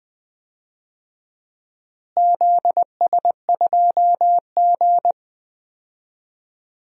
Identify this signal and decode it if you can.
Morse code, heard as a keyed tone, about 20 wpm, 700 Hz. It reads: ZS2G
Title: DXCC-100 Call Signs - 4 Characters - Repeated 2x - 20wpm